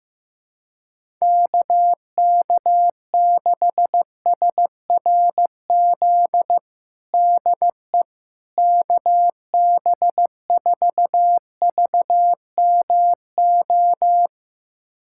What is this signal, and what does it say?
Morse code, heard as a keyed tone, about 15 wpm, 700 Hz. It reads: KK6SRZ DE KB4VMO
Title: USA Callsign DE Callsign - 6 Characters - 15wpm